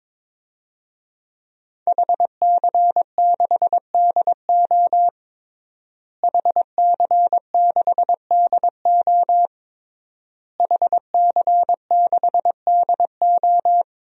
Morse code, heard as a keyed tone, about 22 wpm, 700 Hz. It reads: HC6DO HC6DO HC6DO
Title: DXCC-100 Call Signs - 5 Characters - Repeated 3x - 22wpm